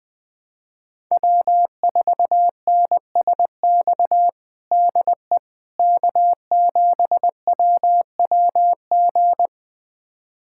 Morse code, heard as a keyed tone, about 20 wpm, 700 Hz. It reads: W4NSX DE K7WWG
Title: USA Callsign DE Callsign 20wpm